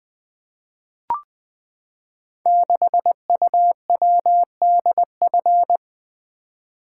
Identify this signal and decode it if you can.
Morse code, heard as a keyed tone, about 20 wpm, 700 Hz. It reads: 6UWDF
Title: Code Groups - 5 Letter:Numbers - 20wpm